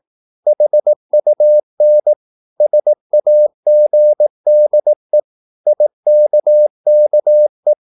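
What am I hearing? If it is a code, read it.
Morse code, heard as a keyed tone, about 18 wpm, 600 Hz. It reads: HUN SAGDE IKKE
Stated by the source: Slow (QRS) Morse practice at 18 wpm, CW med Gnister.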